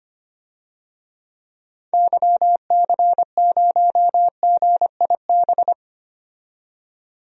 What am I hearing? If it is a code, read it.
Morse code, heard as a keyed tone, about 25 wpm, 690 Hz. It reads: YC0GIB
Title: DXCC-100 Call Signs - 6 Characters - 25wpm